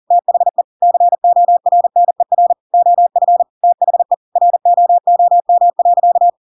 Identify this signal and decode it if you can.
Morse code, heard as a keyed tone, about 40 wpm, 700 Hz. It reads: THE CORNER OF THE ROOM.
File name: Pinocchio / Pinocchio ch7 40 40 ik1hge